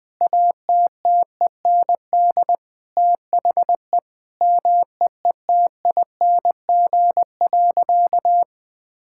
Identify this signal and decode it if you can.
Morse code, heard as a keyed tone, about 21 wpm, 700 Hz. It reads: ATTEND THE MEETING.
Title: Sentences with 6-Word Phrases - 20wpm